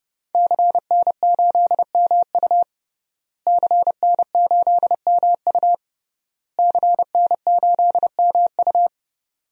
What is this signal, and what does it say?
Morse code, heard as a keyed tone, about 30 wpm, 700 Hz. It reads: CN8MU CN8MU CN8MU
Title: DXCC-100 Call Signs - 5 Characters - Repeated 3x - 30wpm